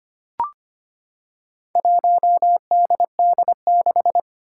Morse code, heard as a keyed tone, about 25 wpm, 700 Hz. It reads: 1DD6
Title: Four Challenging Letter-Number Combinations 25wpm